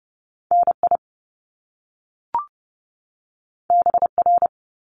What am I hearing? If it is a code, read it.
Morse code, heard as a keyed tone, about 30 wpm, 700 Hz. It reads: NI BR
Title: Top Two Letter Combinations in Words 30wpm